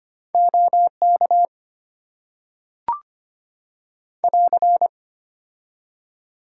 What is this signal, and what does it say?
Morse code, heard as a keyed tone, about 25 wpm, 700 Hz. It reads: OK <AR>
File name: Top 90 QSO Elements - 6x Character Spacing - 25wpm